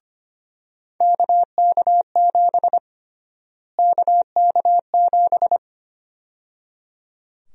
Morse code, heard as a keyed tone, about 25 wpm, 700 Hz. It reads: KK7 KK7
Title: USA Call Sign Prefixes - Repeated 2x - 25wpm